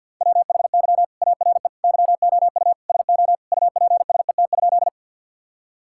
Morse code, heard as a keyed tone, about 51 wpm, 690 Hz. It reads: WHY ARE YOU SO UPSET?
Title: Sentences from Top 1000 Words - Review Part 5 - 50wpm